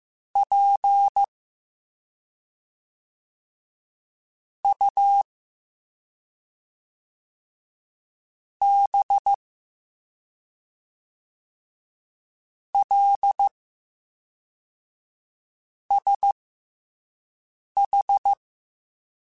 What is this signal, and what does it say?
Morse code, heard as a keyed tone, about 15 wpm, spaced as for 3 wpm, 800 Hz. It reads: PUBLSH